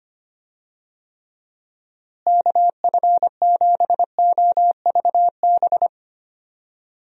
Morse code, heard as a keyed tone, about 25 wpm, 700 Hz. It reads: KF7OVB
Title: USA Call Signs - 6 Characters - Repeated 2x - 25wpm